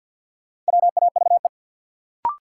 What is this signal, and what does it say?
Morse code, heard as a keyed tone, about 52 wpm, 700 Hz. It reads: WAVE
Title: Sentences from Top 1400 Words - New Word Drill - 50wpm